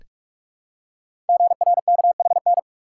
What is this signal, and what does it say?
Morse code, heard as a keyed tone, about 44 wpm, 700 Hz. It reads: GRKHN
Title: Code Groups - 5 Letter:Numbers - 45wpm